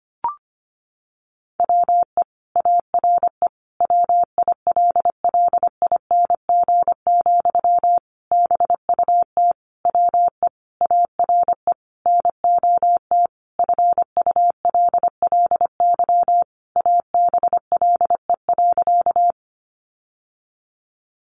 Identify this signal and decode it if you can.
Morse code, heard as a keyed tone, about 25 wpm, 700 Hz. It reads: WE ARE WILLING, BUT WE ARE NOT FULLY ABLE.
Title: Sentences from Top 1300 Words 25wpm